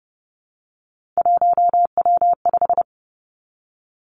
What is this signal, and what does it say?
Morse code, heard as a keyed tone, about 30 wpm, 700 Hz. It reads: 1W5
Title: Three Challenging Letter-Number Combinations 30wpm